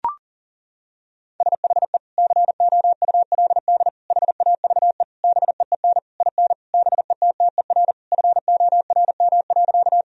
Morse code, heard as a keyed tone, about 37 wpm, 700 Hz. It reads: SHE COULD HAVE BEEN IN BETTER FORM.
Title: Sentences from Top 1000 Words - Review Part 1 - 40wpm